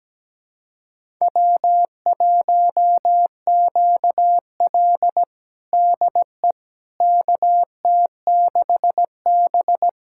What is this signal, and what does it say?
Morse code, heard as a keyed tone, about 17 wpm, 700 Hz. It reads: W1QL DE KT6B
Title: USA Callsign DE Callsign - 4 Characters - 17wpm